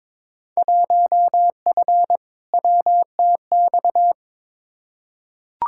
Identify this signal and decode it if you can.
Morse code, heard as a keyed tone, about 22 wpm, 700 Hz. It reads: 1F WTX
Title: Field Day Exchanges Encore - 22wpm